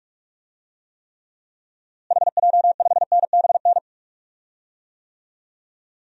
Morse code, heard as a keyed tone, about 42 wpm, 700 Hz. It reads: HJ5NBN
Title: DXCC-100 Call Signs - 6 Characters - Repeated 2x - 45wpm